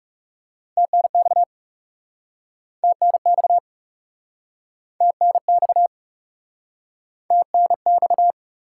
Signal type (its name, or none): Morse code